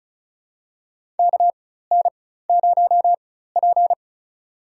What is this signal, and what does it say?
Morse code, heard as a keyed tone, about 36 wpm, spaced as for 16 wpm, 700 Hz. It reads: KN0P
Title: USA Call Signs - 4 Characters - 4x Character Spacing - 35wpm